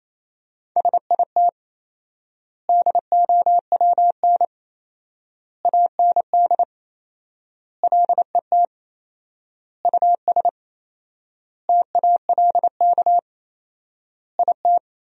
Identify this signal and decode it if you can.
Morse code, heard as a keyed tone, about 28 wpm, 700 Hz. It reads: SIT DOWN AND LET US TALK IT
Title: 4x Word-Spacing - Sentences from Top 500 Words - 28wpm